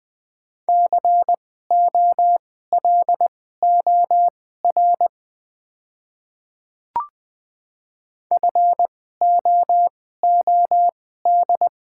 Morse code, heard as a keyed tone, about 20 wpm, spaced as for 14 wpm, 700 Hz. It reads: COLOR FOOD
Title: 2x Character Spacing - Top 500 Words - 20wpm